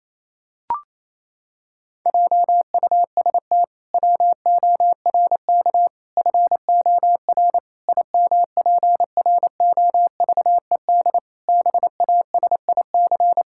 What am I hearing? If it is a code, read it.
Morse code, heard as a keyed tone, about 28 wpm, 700 Hz. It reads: JUST WORK FOR IMPROVED BASIC
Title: Pangram Challenge v2 - 28wpm